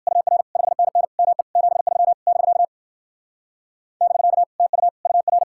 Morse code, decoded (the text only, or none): UR 5NN ME 73 <BK> <BK> TU UR